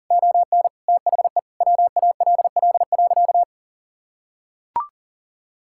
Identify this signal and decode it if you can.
Morse code, heard as a keyed tone, about 37 wpm, 700 Hz. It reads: ON THE WALL.
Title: Sentences from Top 1600 Words 40wpm